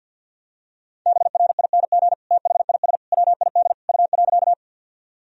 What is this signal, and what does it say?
Morse code, heard as a keyed tone, about 47 wpm, 700 Hz. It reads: BRING THIS WID U.
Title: Sentences from Top 500 Words and Abbreviations 50wpm